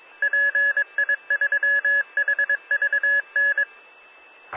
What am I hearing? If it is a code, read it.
Morse code, heard as a keyed tone, about 22 wpm, 1700 Hz, with radio noise. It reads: PI3HVN